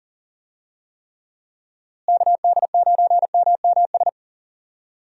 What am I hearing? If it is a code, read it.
Morse code, heard as a keyed tone, about 37 wpm, 700 Hz. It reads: KD9MMS
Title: USA Call Signs - 6 Characters - Repeated 3x - 40wpm